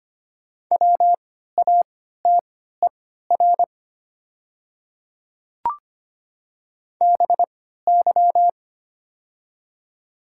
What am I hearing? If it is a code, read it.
Morse code, heard as a keyed tone, about 25 wpm, spaced as for 14 wpm, 700 Hz. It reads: WATER BY
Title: 3x Character Spacing - Top 100 Words - 25wpm